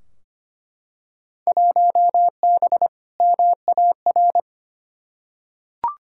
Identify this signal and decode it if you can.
Morse code, heard as a keyed tone, about 25 wpm, 700 Hz. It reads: 1B MAR
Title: Field Day Exchanges Encore - 25wpm